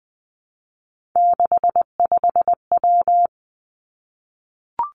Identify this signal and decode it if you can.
Morse code, heard as a keyed tone, about 20 wpm, 700 Hz. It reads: 65W
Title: Three Challenging Letter-Number Combinations 20wpm